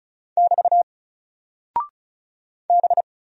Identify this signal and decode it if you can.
Morse code, heard as a keyed tone, about 36 wpm, 700 Hz. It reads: <BT> B
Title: Single Character (Letter:Number:Punctuation:Prosign) Alt 35wpm